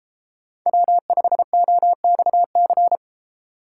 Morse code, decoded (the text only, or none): W5OXC